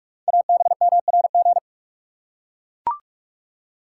Morse code, heard as a keyed tone, about 44 wpm, 700 Hz. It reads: ABMRG
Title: Code Groups - 5 Letters - 45wpm